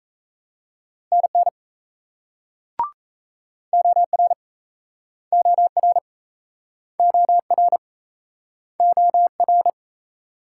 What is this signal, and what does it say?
Morse code, heard as a keyed tone, about 39 wpm, 700 Hz. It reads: NN OR OR OR OR